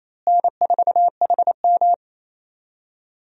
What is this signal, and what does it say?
Morse code, heard as a keyed tone, about 28 wpm, 700 Hz. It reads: N4HM
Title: USA Call Signs - 4 Characters - Repeated 3x - 28wpm